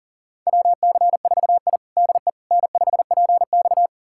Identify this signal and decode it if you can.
Morse code, heard as a keyed tone, about 40 wpm, 700 Hz. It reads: WC4I DE N5PX
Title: USA Callsign DE Callsign - 4 Characters - 40wpm